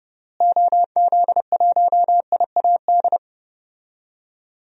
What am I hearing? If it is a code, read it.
Morse code, heard as a keyed tone, about 30 wpm, 700 Hz. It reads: OZ1IAD
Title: DXCC-100 Call Signs - 6 Characters - Repeated 2x - 30wpm